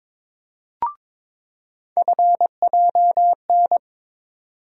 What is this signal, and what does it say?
Morse code, heard as a keyed tone, about 22 wpm, 700 Hz. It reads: FJN
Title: USA Call Sign Postfixes - 3 Letters 22wpm